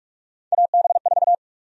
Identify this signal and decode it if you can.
Morse code, heard as a keyed tone, about 45 wpm, 700 Hz. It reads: AB4
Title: USA Call Sign Prefixes - 45wpm